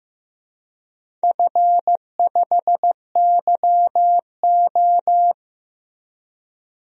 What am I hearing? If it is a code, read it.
Morse code, heard as a keyed tone, about 15 wpm, 700 Hz. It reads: F5YO
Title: DXCC-100 Call Signs - 4 Characters - Repeated 2x - 15wpm